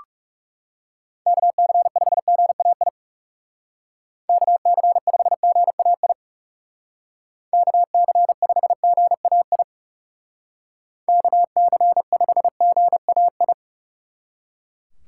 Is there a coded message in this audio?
Yes: Morse code from 1.3 to 13.5 s, heard as a keyed tone, about 39 wpm, 700 Hz. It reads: KC5GAI KC5GAI KC5GAI KC5GAI